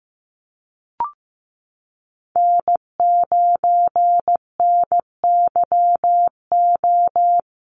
Morse code, heard as a keyed tone, about 15 wpm, 700 Hz. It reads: N9NYO
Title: USA Call Signs Repeated 2x 15wpm